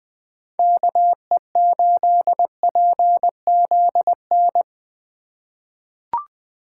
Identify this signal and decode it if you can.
Morse code, heard as a keyed tone, about 20 wpm, 700 Hz. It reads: KE8PZN
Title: 2023 POTA Activator Call Signs - 20wpm